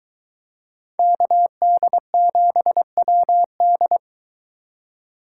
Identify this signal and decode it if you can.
Morse code, heard as a keyed tone, about 21 wpm, 700 Hz. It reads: KD7WD